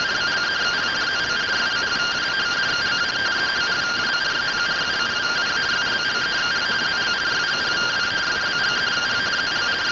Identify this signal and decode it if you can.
Two-tone teleprinter signal, about 170 Hz shift.